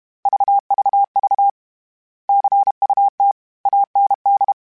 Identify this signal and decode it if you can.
Morse code, heard as a keyed tone, about 32 wpm, 800 Hz. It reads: VVV CUT AND
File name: Misc / Misc-mp3s / Word Pairs @30